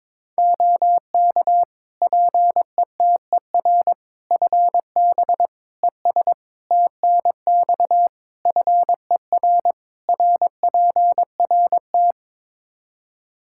Morse code, heard as a keyed tone, about 22 wpm, 700 Hz. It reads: OK PETER FB ES TNX FER RPRT
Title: K1USN SST QSO Practice 22wpm